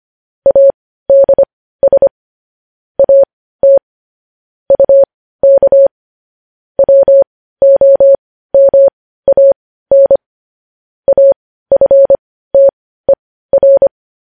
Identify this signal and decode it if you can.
Morse code, heard as a keyed tone, about 25 wpm, spaced as for 15 wpm, 550 Hz. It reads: ADS AT UK WOMAN AFTER
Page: news headlines in morse code at 15 wpm